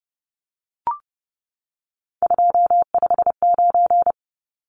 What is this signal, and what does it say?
Morse code, heard as a keyed tone, about 30 wpm, 700 Hz. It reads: 259
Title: Numbers - 3-Digit 30wpm